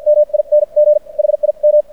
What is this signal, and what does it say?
Morse code, heard as a keyed tone, about 56 wpm, spaced as for 43 wpm, 590 Hz. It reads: MING HIM
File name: archived / CRSnet